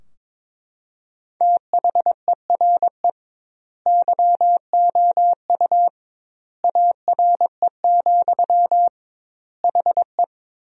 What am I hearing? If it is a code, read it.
Morse code, heard as a keyed tone, about 22 wpm, 700 Hz. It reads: THERE YOU ARE, HE
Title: 2x Word-Spacing - Sentences from Top 100 Words - 22wpm